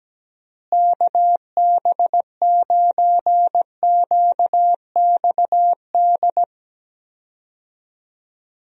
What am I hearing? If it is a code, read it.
Morse code, heard as a keyed tone, about 17 wpm, 700 Hz. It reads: KB9QXD
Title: USA Call Signs - 6 Characters - Repeated 3x - 17wpm